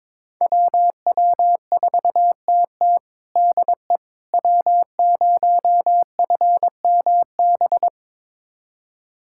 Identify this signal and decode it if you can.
Morse code, heard as a keyed tone, about 22 wpm, 700 Hz. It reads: WW4TT DE W0FMB
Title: USA Callsign DE Callsign - 5 Characters - 22wpm